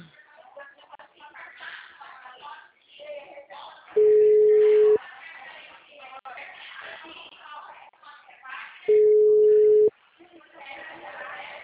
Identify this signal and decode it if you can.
Ringback tone (Europe, 425 Hz), with speech.